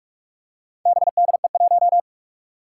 Morse code, heard as a keyed tone, about 46 wpm, 700 Hz. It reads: BDE1
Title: Four Challenging Letter-Number Combinations 45wpm